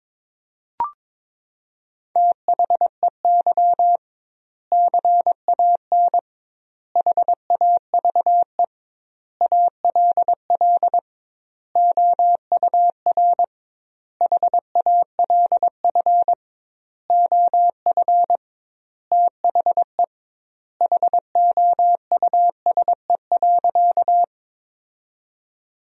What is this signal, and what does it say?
Morse code, heard as a keyed tone, about 21 wpm, 700 Hz. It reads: THEY CAN HAVE ALL OUR HALF OF THE HOUSE.
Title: 2x Word-Spacing - Sentences from Top 400 Words - 22wpm